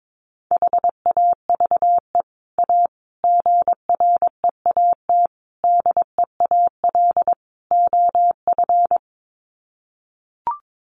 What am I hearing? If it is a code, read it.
Morse code, heard as a keyed tone, about 22 wpm, 700 Hz. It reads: HAVE A GREAT DEAL OF
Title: Sets of 5 Words Encore2 22wpm